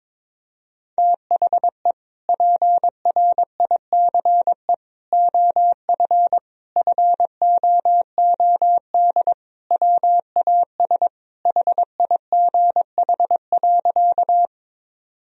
Morse code, heard as a keyed tone, about 22 wpm, 700 Hz. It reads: THE PRICE OF FOOD WAS HIGH.
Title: Sentences from Top 1000 Words - Review Part 4 - 22wpm